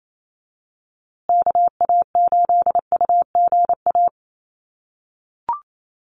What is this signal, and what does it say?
Morse code, heard as a keyed tone, about 28 wpm, 700 Hz. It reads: KA8UGA
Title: USA Call Signs Repeated 2x 28wpm